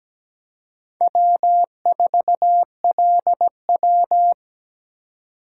Morse code, heard as a keyed tone, about 20 wpm, 700 Hz. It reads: W4LW
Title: Speed Racing - USA 1x2 Call Signs 17wpm